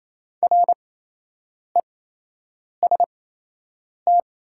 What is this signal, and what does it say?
Morse code, heard as a keyed tone, about 28 wpm, spaced as for 8 wpm, 700 Hz. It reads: REST
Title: 8x Character Spacing - Top 500 Words - 28wpm